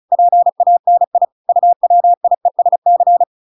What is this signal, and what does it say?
Morse code, heard as a keyed tone, about 35 wpm, 700 Hz. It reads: PANI UWIESC